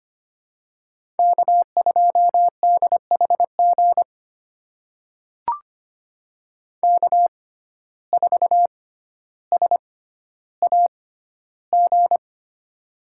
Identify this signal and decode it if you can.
Morse code, heard as a keyed tone, about 25 wpm, spaced as for 13 wpm, 700 Hz. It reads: K2DHG K4SAG
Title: USA Call Signs - 5 Characters - 6x Character Spacing - 25wpm